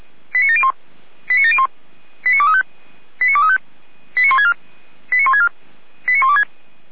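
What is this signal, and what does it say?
Five-tone selective call (ZVEI), heard as quick runs of five tones, tones of about 70 ms. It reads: 89701 89701 89235 89235 89164 89164 89126
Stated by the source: ZVEI five-tone selective call received by SDR.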